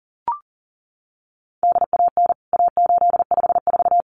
Morse code, heard as a keyed tone, about 40 wpm, 700 Hz. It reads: DAN A854
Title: CWT Exchange 40wpm